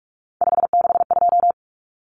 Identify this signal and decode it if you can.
Morse code, heard as a keyed tone, about 44 wpm, 700 Hz. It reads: 562
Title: Common Cut Numbers - 3-Digit 45wpm